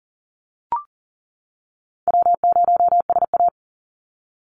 Morse code, heard as a keyed tone, about 40 wpm, 700 Hz. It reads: W0SA
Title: Call Signs - 40wpm